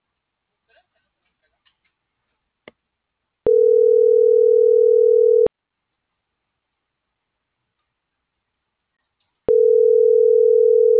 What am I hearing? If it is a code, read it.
Ringback tone (North America), 440 + 480 Hz.